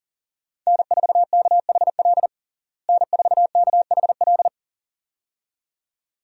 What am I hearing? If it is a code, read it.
Morse code, heard as a keyed tone, about 37 wpm, 700 Hz. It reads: N4KHL N4KHL
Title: USA Call Signs - 5 Characters - Repeated 2x - 40wpm